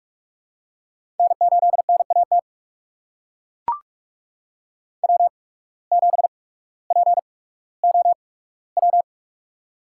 Morse code, heard as a keyed tone, about 43 wpm, spaced as for 19 wpm, 700 Hz. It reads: N8NAT W7POW
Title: USA Call Signs - 8x Character Spacing - 45wpm